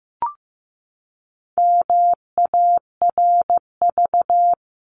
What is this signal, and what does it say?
Morse code, heard as a keyed tone, about 15 wpm, 700 Hz. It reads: MARV
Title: CWT Exchange 15wpm